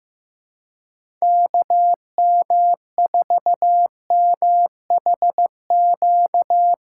Morse code, heard as a keyed tone, about 15 wpm, 700 Hz. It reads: KM4MHQ
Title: USA Callsign DE Callsign - 6 Characters - 15wpm